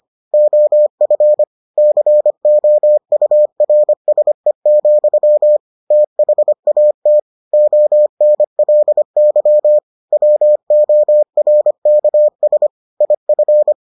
Morse code, heard as a keyed tone, about 25 wpm, 600 Hz. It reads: OF COURSE, THAT ONLY WORKS IF